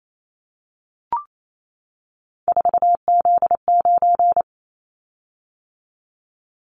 Morse code, heard as a keyed tone, about 28 wpm, 700 Hz. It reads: 4Z9
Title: Callsign Prefixes for Top 50 DXCC - 28wpm